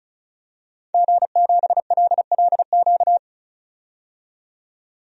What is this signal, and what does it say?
Morse code, heard as a keyed tone, about 36 wpm, 700 Hz. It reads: G7LLQ
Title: DXCC-100 Call Signs - 5 Characters - 35wpm